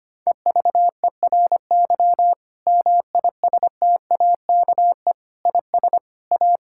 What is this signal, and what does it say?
Morse code, heard as a keyed tone, about 25 wpm, 700 Hz. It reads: EVERY MISTAKE IS A
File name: practice / 25wpm / Difficult Sentences with Hints - 25wpm